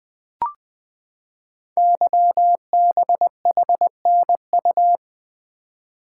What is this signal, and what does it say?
Morse code, heard as a keyed tone, about 20 wpm, 700 Hz. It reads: YBHNU
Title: Code Groups - 5 Letter:Numbers - 20wpm